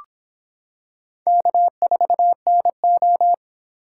Morse code, heard as a keyed tone, about 26 wpm, 700 Hz. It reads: K4NO